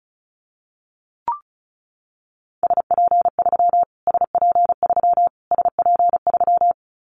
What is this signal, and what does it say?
Morse code, heard as a keyed tone, about 34 wpm, 700 Hz. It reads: SP3 SP3 SP3
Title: Callsign Prefixes for Top 25 DXCC - Repeated 3x - 35wpm